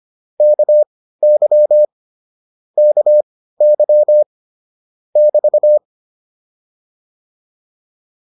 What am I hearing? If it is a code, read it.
Morse code, heard as a keyed tone, about 25 wpm, spaced as for 15 wpm, 600 Hz. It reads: KY KY <BT>